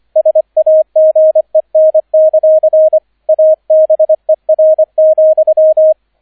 Morse code, heard as a keyed tone, about 25 wpm, 630 Hz, with radio noise. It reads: SAGEN; ABER,
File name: 003 Dritter Streich 25wpm